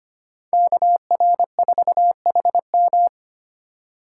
Morse code, heard as a keyed tone, about 32 wpm, 700 Hz. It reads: KR4HM